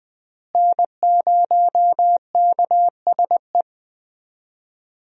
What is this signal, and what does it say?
Morse code, heard as a keyed tone, about 20 wpm, 700 Hz. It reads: N0KSE
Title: USA Call Signs - 5 Characters - Repeated 2x - 20wpm